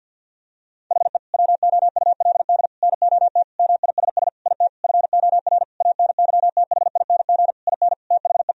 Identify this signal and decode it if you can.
Morse code, heard as a keyed tone, about 50 wpm, 700 Hz. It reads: HE WOULD NOT MISS IT FOR ANYTHING IN THE
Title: Sentences from Top 1000 Words - Review Part 3 - 50wpm